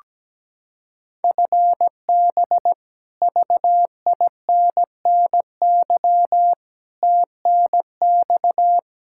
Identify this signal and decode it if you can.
Morse code, heard as a keyed tone, about 17 wpm, 700 Hz. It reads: FB VINNY TNX
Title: Practice QSO - Exchange 3 - More Info - 17wpm